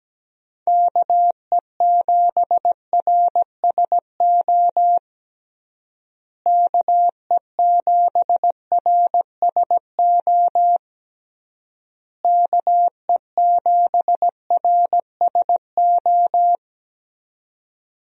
Morse code, heard as a keyed tone, about 17 wpm, 700 Hz. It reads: KE7RSO KE7RSO KE7RSO
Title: USA Call Signs - 6 Characters - Repeated 3x - 17wpm